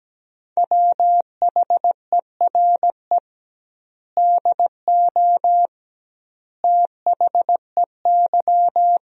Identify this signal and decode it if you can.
Morse code, heard as a keyed tone, about 17 wpm, 700 Hz. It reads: WHERE DO THEY
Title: 2x Word-Spacing - Sentences from Top 200 Words - 17wpm